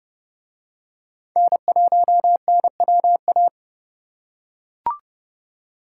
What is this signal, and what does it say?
Morse code, heard as a keyed tone, about 28 wpm, 700 Hz. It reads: N1NWA